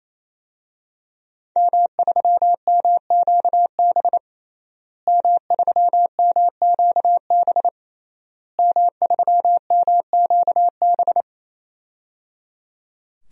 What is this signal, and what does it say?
Morse code, heard as a keyed tone, about 27 wpm, 700 Hz. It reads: M3MQB M3MQB M3MQB
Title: DXCC-100 Call Signs - 5 Characters - Repeated 3x - 28wpm